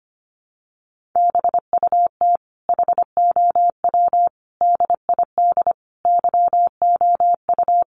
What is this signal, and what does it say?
Morse code, heard as a keyed tone, about 25 wpm, 700 Hz. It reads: BUT HOW DID YOU
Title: Sentences from Top 1400 Words 25wpm